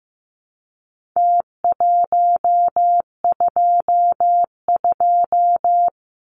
Morse code, heard as a keyed tone, about 15 wpm, 700 Hz. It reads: T122
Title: Numbers - Contest Style - 4-Digit 15wpm